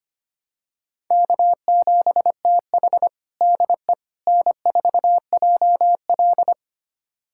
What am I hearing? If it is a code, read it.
Morse code, heard as a keyed tone, about 25 wpm, 700 Hz. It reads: K7TH DE N4JL
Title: USA Callsign DE Callsign - 4 Characters - 25wpm